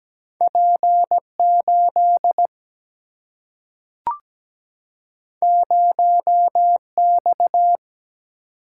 Morse code, heard as a keyed tone, about 17 wpm, 700 Hz. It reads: P 8 0 X
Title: Single-Letter Single-Number 17wpm